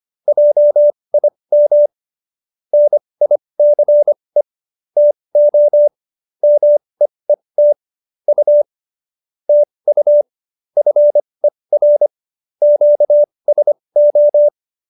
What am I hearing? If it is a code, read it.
Morse code, heard as a keyed tone, about 25 wpm, spaced as for 20 wpm, 590 Hz. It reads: JIM NICE TO MEET U TU FER QSO
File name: QSO307 20